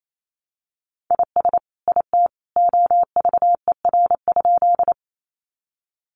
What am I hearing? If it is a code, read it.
Morse code, heard as a keyed tone, about 28 wpm, 700 Hz. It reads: IS IT OVER?
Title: Sentences from Top 100 Words 28wpm